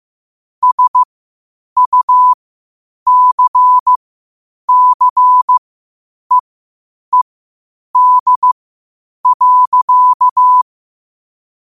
Morse code, heard as a keyed tone, about 15 wpm, spaced as for 8 wpm, 1000 Hz. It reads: SUCCEED.